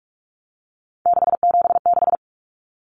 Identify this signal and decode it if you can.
Morse code, heard as a keyed tone, about 44 wpm, 700 Hz. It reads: 676
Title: Numbers - 3-Digit 45wpm